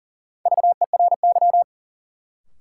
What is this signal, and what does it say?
Morse code, heard as a keyed tone, about 40 wpm, 700 Hz. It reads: VERY
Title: Sets of 2 Words - New Word Drill - 40wpm